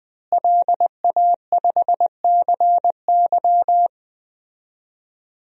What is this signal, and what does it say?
Morse code, heard as a keyed tone, about 21 wpm, 700 Hz. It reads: LA5CY
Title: DXCC-100 Call Signs - 5 Characters - Repeated 3x - 20wpm